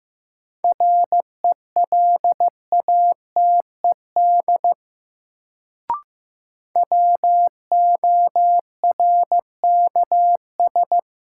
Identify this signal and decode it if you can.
Morse code, heard as a keyed tone, about 15 wpm, 700 Hz. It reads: RELATED WORKS
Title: Sentences from Top 1000 Words - Review Part 5 - New Word Drill - 15wpm